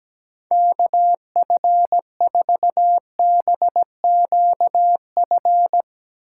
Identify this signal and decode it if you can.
Morse code, heard as a keyed tone, about 17 wpm, 700 Hz. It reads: KF4BQF
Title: Speed Racing - USA 2x3 Call Signs 17wpm